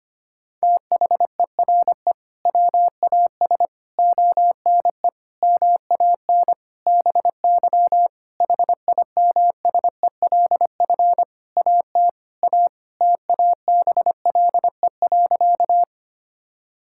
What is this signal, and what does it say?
Morse code, heard as a keyed tone, about 25 wpm, 700 Hz. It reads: THERE WAS ONE MAN BY HIMSELF AT A TABLE.
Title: Sentences from Top 1600 Words 25wpm